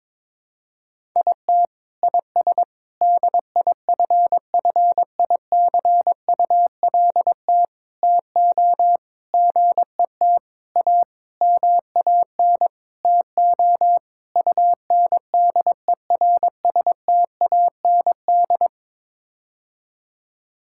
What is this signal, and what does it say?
Morse code, heard as a keyed tone, about 22 wpm, 700 Hz. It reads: IT IS DIFFICULT TO GET A MAN TO UNDERSTAND
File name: Sets of 9 Words - 22wpm